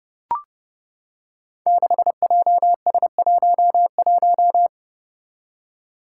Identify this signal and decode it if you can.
Morse code, heard as a keyed tone, about 30 wpm, 700 Hz. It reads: 6JS11
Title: Five Challenging Letter-Number Combinations 30wpm